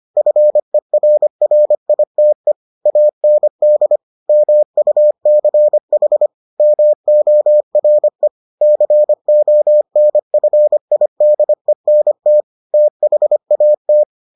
Morse code, heard as a keyed tone, about 25 wpm, 600 Hz. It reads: FERRITE AND MUCH MORE CONFIDENT THAT